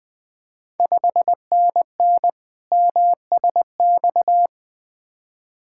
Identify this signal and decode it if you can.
Morse code, heard as a keyed tone, about 20 wpm, 700 Hz. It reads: 5NN MSX
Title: Ontario QSO Party Exchange 20wpm